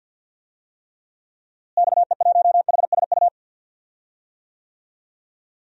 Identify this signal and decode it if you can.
Morse code, heard as a keyed tone, about 52 wpm, 700 Hz. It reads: XE1HSU
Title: DXCC-100 Call Signs - 6 Characters - Repeated 3x - 50wpm